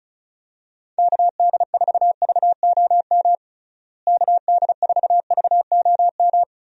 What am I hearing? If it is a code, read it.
Morse code, heard as a keyed tone, about 34 wpm, 700 Hz. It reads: KD4VOM KD4VOM
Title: USA Call Signs - 6 Characters - Repeated 3x - 35wpm